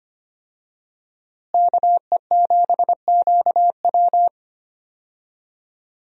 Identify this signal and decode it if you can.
Morse code, heard as a keyed tone, about 25 wpm, 700 Hz. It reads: KE7QW
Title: USA Call Signs - 5 Characters - Repeated 2x - 25wpm